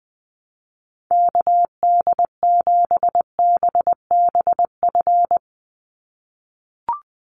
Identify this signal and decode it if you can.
Morse code, heard as a keyed tone, about 20 wpm, 700 Hz. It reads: KD7BBF
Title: USA Call Signs 20wpm